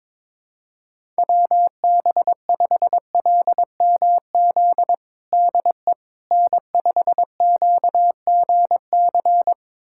Morse code, heard as a keyed tone, about 22 wpm, 700 Hz. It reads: WB5LMZ DE N5QGC
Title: USA Callsign DE Callsign 22wpm